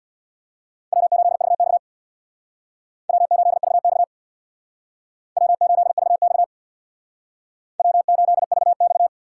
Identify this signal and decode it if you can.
Morse code, heard as a keyed tone, about 62 wpm, 700 Hz. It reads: W8VX W8VX W8VX W8VX
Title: Speed Racing - USA 1x2 Call Signs 50wpm